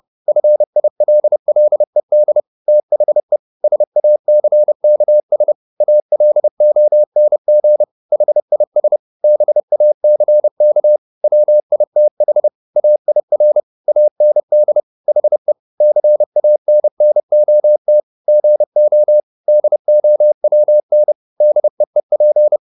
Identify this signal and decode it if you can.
Morse code, heard as a keyed tone, about 30 wpm, 600 Hz. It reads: FILLED THE SACKS ALONG HIS BACK WITH AIR AND HE CANNOT GO DOWN DEEP